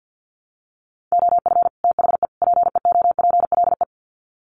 Morse code, heard as a keyed tone, about 50 wpm, 700 Hz. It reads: OF THE PEOPLE
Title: Sets of 3 Words 50wpm